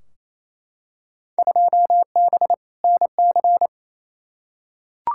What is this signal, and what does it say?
Morse code, heard as a keyed tone, about 28 wpm, 700 Hz. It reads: 2B NC
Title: Field Day Exchanges Encore - 28wpm